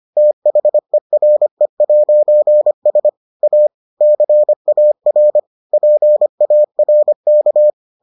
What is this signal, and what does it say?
Morse code, heard as a keyed tone, about 25 wpm, 600 Hz. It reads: THERE'S A CAR PARK